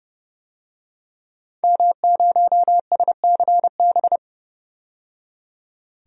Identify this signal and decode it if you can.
Morse code, heard as a keyed tone, about 30 wpm, 700 Hz. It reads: M0SCB
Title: DXCC-100 Call Signs - 5 Characters - 30wpm